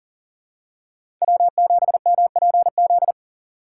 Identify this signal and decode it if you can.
Morse code, heard as a keyed tone, about 40 wpm, 700 Hz. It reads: W7MPZ
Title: USA Call Signs - 5 Characters - 2x Character Spacing - 40wpm